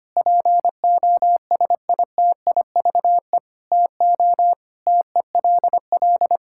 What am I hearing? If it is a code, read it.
Morse code, heard as a keyed tone, about 25 wpm, 700 Hz. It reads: POSITIVE TO TELL
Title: Sentences from Top 1000 Words - Review Part 5 - 25wpm